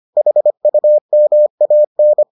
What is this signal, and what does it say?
Morse code, heard as a keyed tone, about 25 wpm, 600 Hz. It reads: HUMAN